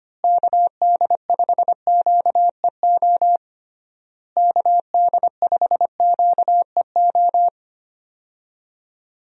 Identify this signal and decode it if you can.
Morse code, heard as a keyed tone, about 25 wpm, 700 Hz. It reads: KD5QEO KD5QEO
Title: USA Call Signs - 6 Characters - Repeated 3x - 25wpm